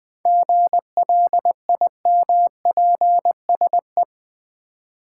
Morse code, heard as a keyed tone, about 20 wpm, 700 Hz. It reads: GLIMPSE
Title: 7 Letter Words - First 3 Letters Given - 20wpm